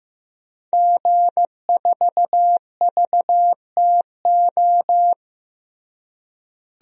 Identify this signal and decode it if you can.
Morse code, heard as a keyed tone, about 15 wpm, 700 Hz. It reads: G4VTO
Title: DXCC-100 Call Signs - 5 Characters - Repeated 3x - 15wpm